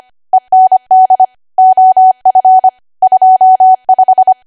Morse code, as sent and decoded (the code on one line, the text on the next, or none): . -. -.. / --- ..-. / ..--- .....
END OF 25